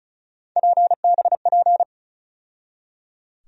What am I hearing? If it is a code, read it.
Morse code, heard as a keyed tone, about 36 wpm, 700 Hz. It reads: PBP
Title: USA Call Sign Postfixes - 3 Letters 35wpm